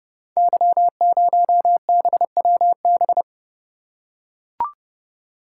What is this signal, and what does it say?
Morse code, heard as a keyed tone, about 30 wpm, 700 Hz. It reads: Y0BWB
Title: Code Groups - 5 Letter:Numbers - 30wpm